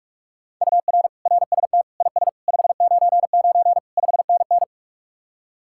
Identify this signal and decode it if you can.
Morse code, heard as a keyed tone, about 49 wpm, 700 Hz. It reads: UR RST IS 599 5NN